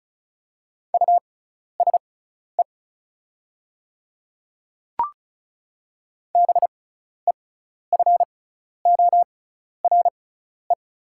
Morse code, heard as a keyed tone, about 36 wpm, spaced as for 12 wpm, 700 Hz. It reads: USE BEFORE
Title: 6x Character Spacing - Top 200 Words - 35wpm